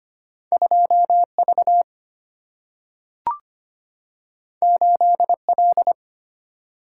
Morse code, heard as a keyed tone, about 25 wpm, 700 Hz. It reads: 2V 8L
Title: Single-Letter Single-Number 25wpm